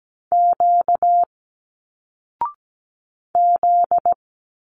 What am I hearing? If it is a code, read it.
Morse code, heard as a keyed tone, about 17 wpm, 700 Hz. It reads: Q Z
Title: Single Letters 17wpm